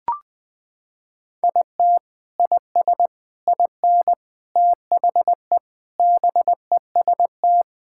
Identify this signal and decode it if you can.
Morse code, heard as a keyed tone, about 21 wpm, 700 Hz. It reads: IT IS IN THE BEST
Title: Sentences with 7-Word Phrases - 20wpm